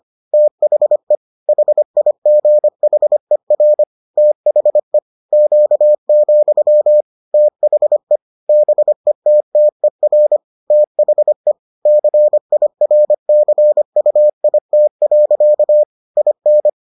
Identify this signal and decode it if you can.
Morse code, heard as a keyed tone, about 25 wpm, 600 Hz. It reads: THE HIGHER THE Q, THE BETTER THE CIRCUIT. IN